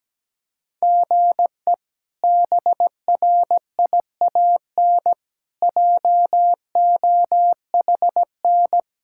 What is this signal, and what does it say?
Morse code, heard as a keyed tone, about 17 wpm, 700 Hz. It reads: GE BRIAN JOHN
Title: K1USN SST QSO Practice - 17wpm